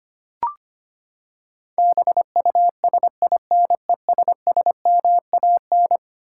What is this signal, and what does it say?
Morse code, heard as a keyed tone, about 25 wpm, 700 Hz. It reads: BUSINESSMAN
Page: morse-code-ninja-practice-25wpm